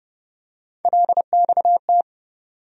Morse code, heard as a keyed tone, about 29 wpm, 700 Hz. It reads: LXT
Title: USA Call Sign Postfixes - 3 Letters 30wpm